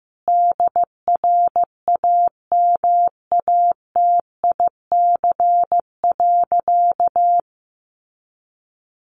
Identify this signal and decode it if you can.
Morse code, heard as a keyed tone, about 15 wpm, 700 Hz. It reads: DRAMATIC.
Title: Sentences from Top 1300 Words 15wpm